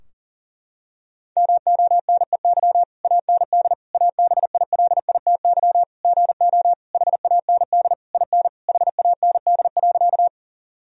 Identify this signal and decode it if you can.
Morse code, heard as a keyed tone, about 37 wpm, 700 Hz. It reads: MONEY AND ABILITY GO HAND IN HAND.
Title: Sentences from Top 1000 Words - Review Part 3 - 40wpm